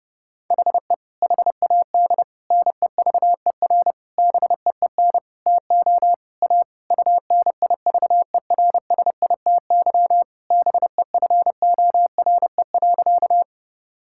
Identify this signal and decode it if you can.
Morse code, heard as a keyed tone, about 30 wpm, 700 Hz. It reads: HE HAD NEVER BEEN TO A UNIVERSITY BEFORE.
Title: Sentences from Top 1000 Words - Review Part 5 - 30wpm